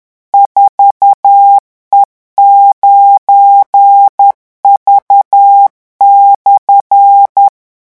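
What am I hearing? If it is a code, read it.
Morse code, heard as a keyed tone, about 10 wpm, 800 Hz. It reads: 4E9V/
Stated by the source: French Morse code revision lesson 18 at 10 words per minute.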